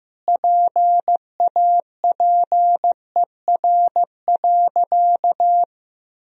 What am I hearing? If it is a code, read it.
Morse code, heard as a keyed tone, about 16 wpm, 700 Hz. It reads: PAPER.
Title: Sentences from Top 1000 Words - Review Part 2 - 15wpm